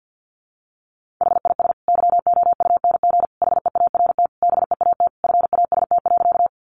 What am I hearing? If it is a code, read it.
Morse code, heard as a keyed tone, about 50 wpm, 700 Hz. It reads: HIS YOUNG HEART BEAT FAST.
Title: Sentences from Top 1300 Words 50wpm